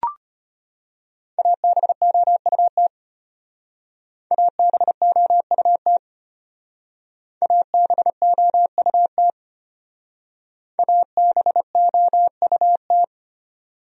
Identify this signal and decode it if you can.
Morse code, heard as a keyed tone, about 32 wpm, 700 Hz. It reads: ABOUT ABOUT ABOUT ABOUT